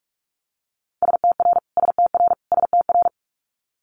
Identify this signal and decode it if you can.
Morse code, heard as a keyed tone, about 46 wpm, 700 Hz. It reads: STR STR STR
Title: Multi-letter Phonemes Repeated 3x 45wpm